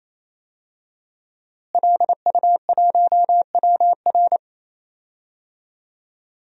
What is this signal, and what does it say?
Morse code, heard as a keyed tone, about 28 wpm, 700 Hz. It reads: LU1WR
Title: DXCC-100 Call Signs - 5 Characters - Repeated 3x - 28wpm